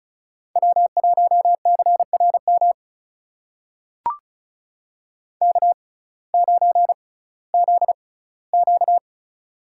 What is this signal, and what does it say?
Morse code, heard as a keyed tone, about 34 wpm, spaced as for 19 wpm, 700 Hz. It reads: W1CRM K9ZQ
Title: USA Call Signs - 6x Character Spacing - 35wpm